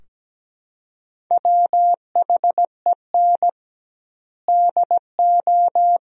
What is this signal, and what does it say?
Morse code, heard as a keyed tone, about 17 wpm, 700 Hz. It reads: WHEN DO
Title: 2x Word-Spacing - Sentences from Top 400 Words - 17wpm